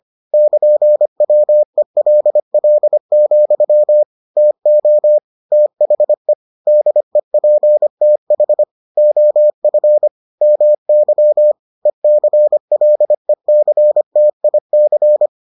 Morse code, heard as a keyed tone, about 25 wpm, 600 Hz. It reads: <KN>WELL, TO THE DEPTH OF MY ECLECTIC